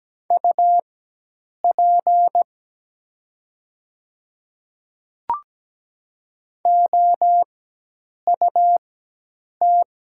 Morse code, heard as a keyed tone, about 17 wpm, spaced as for 8 wpm, 700 Hz. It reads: UP OUT
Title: 4x Character Spacing - Top 200 Words - 17wpm